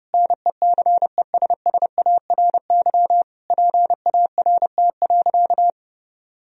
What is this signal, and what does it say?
Morse code, heard as a keyed tone, about 30 wpm, 700 Hz. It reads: NECESSARY PART.
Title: Sentences from Top 1000 Words - Review Part 1 - 30wpm